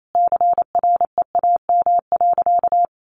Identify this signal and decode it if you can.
Morse code, heard as a keyed tone, about 28 wpm, 700 Hz. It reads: CREAM.
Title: Sentences from Top 1500 Words 28wpm